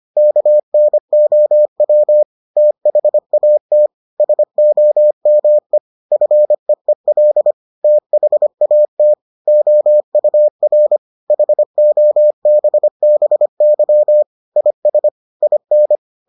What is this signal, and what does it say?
Morse code, heard as a keyed tone, about 25 wpm, 600 Hz. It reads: KNOW THAT SOME FEEL THAT OUR HOBBY IS IN